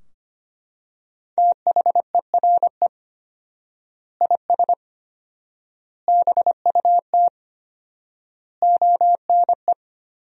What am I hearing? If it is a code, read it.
Morse code, heard as a keyed tone, about 25 wpm, 700 Hz. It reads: THERE IS BUT ONE
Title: 4x Word-Spacing - Sentences from Top 100 Words - 25wpm